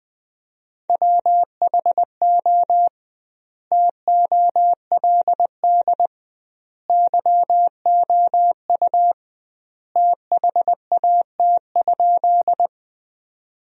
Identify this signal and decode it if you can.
Morse code, heard as a keyed tone, about 21 wpm, 700 Hz. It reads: WHO TOLD YOU THAT?
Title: 2x Word-Spacing - Sentences from Top 400 Words - 20wpm